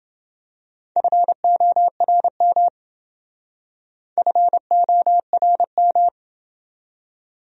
Morse code, heard as a keyed tone, about 28 wpm, 700 Hz. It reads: FORM FORM